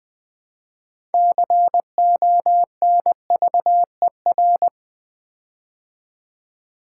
Morse code, heard as a keyed tone, about 20 wpm, 700 Hz. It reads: CONVER
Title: Build A Word - 6 Letters At A Time - 12-Letter Words - 20wpm